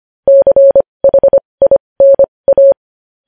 Morse code, heard as a keyed tone, about 25 wpm, spaced as for 20 wpm, 550 Hz. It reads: CHINA
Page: news headlines in morse code at 20 wpm